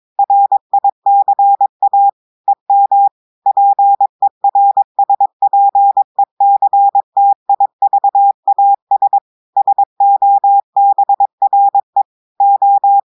Morse code, heard as a keyed tone, about 22 wpm, 800 Hz. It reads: RICA EM PERSPECTIVAS SOBRE O
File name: audios / portugais / portugais texte en clair 1 vitesse 22 mots minutes